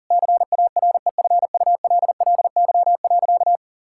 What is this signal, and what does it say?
Morse code, heard as a keyed tone, about 40 wpm, 700 Hz. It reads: CAREFULLY.
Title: Sentences from Top 1600 Words 40wpm